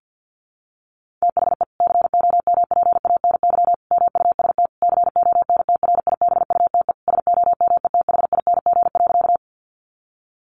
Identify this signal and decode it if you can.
Morse code, heard as a keyed tone, about 50 wpm, 700 Hz. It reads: THE COMPANY MUST CONTRIBUTE SOMETHING.
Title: Sentences from Top 1400 Words 50wpm